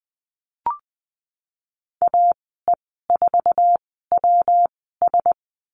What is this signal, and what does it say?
Morse code, heard as a keyed tone, about 20 wpm, spaced as for 15 wpm, 700 Hz. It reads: AE4WS
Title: 154 Lesson 25g - Letter p - Call Signs 20wpm